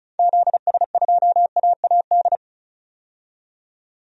Morse code, heard as a keyed tone, about 34 wpm, 700 Hz. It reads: ZS2AAD
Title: DXCC-100 Call Signs - 6 Characters - 35wpm